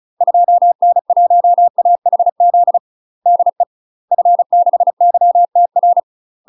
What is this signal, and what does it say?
Morse code, heard as a keyed tone, about 36 wpm, 700 Hz. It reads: 2N1AHZ DE F6YTR
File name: UK MASTER QSOs TRAINING / 35WPM / UK MASTER QSOs 0024